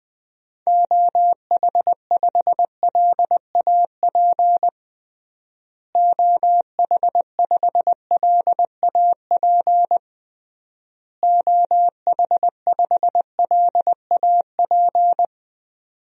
Morse code, heard as a keyed tone, about 21 wpm, 700 Hz. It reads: OH5LAP OH5LAP OH5LAP
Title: DXCC-100 Call Signs - 6 Characters - Repeated 3x - 20wpm